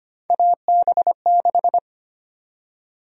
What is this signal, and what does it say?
Morse code, heard as a keyed tone, about 25 wpm, 700 Hz. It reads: AB6
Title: USA Call Sign Prefixes - 25wpm